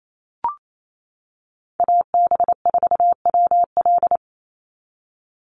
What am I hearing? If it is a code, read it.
Morse code, heard as a keyed tone, about 28 wpm, 700 Hz. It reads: AB4WL
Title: NA SOTA Chaser Callsigns 28wpm